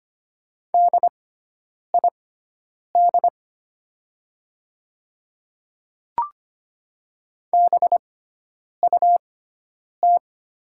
Morse code, heard as a keyed tone, about 25 wpm, spaced as for 9 wpm, 700 Hz. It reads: DID BUT
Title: 6x Character Spacing - Top 100 Words - 25wpm